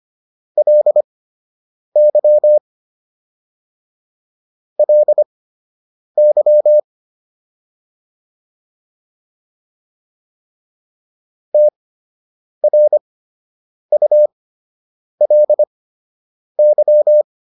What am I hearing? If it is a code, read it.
Morse code, heard as a keyed tone, about 25 wpm, spaced as for 8 wpm, 600 Hz. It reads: LY LY TRULY